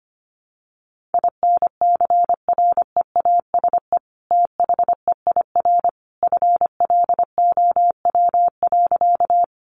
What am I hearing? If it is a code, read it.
Morse code, heard as a keyed tone, about 25 wpm, 700 Hz. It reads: INCREASE THEIR FLOW.
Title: Sentences from Top 1200 Words 25wpm